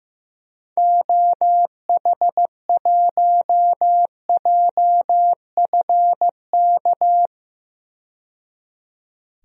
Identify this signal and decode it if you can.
Morse code, heard as a keyed tone, about 15 wpm, 700 Hz. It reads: OH1JFK
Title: DXCC-100 Call Signs - 6 Characters - 15wpm